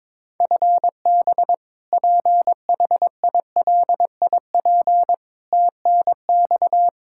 Morse code, heard as a keyed tone, about 22 wpm, 700 Hz. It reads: FB PHILIP TNX
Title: Practice QSO - Exchange 3 - More Info - 22wpm